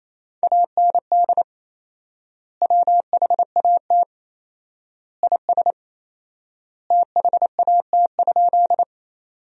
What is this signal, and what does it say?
Morse code, heard as a keyed tone, about 28 wpm, 700 Hz. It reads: AND WHAT IS THAT?
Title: 4x Word-Spacing - Sentences from Top 100 Words - 28wpm